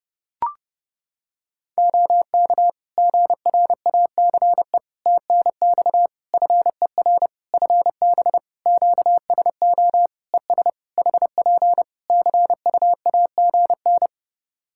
Morse code, heard as a keyed tone, about 29 wpm, 700 Hz. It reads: OK GRACE TNX FER FB QSO ES HP CUAGN
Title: Practice QSO - Exchange 4 - Ending - 30wpm